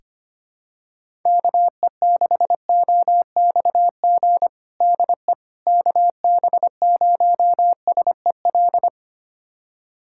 Morse code, heard as a keyed tone, about 25 wpm, 700 Hz. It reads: KE6OXG DE KB0SEL
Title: USA Callsign DE Callsign - 6 Characters - 25wpm